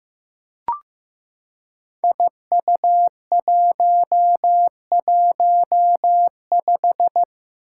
Morse code, heard as a keyed tone, about 15 wpm, 700 Hz. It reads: IU115
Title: Five Challenging Letter-Number Combinations 15wpm